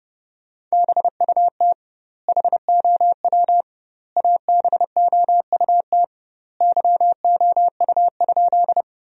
Morse code, heard as a keyed tone, about 30 wpm, 700 Hz. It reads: BUT HOW ABOUT YOU?
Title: 2x Word-Spacing - Sentences from Top 100 Words - 30wpm